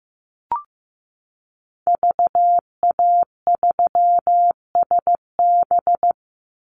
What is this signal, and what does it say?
Morse code, heard as a keyed tone, about 15 wpm, 700 Hz. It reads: VA3SB
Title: Call Signs Repeated 2x - 15wpm